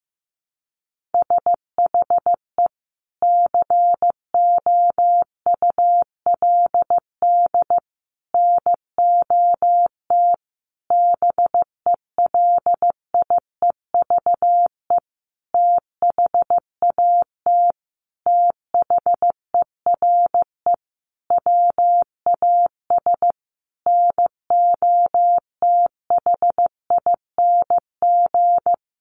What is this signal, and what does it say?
Morse code, heard as a keyed tone, about 15 wpm, 700 Hz. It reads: SHE COULD NOT BELIEVE THAT THERE WAS NOTHING
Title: Sentences from Top 1000 Words 15wpm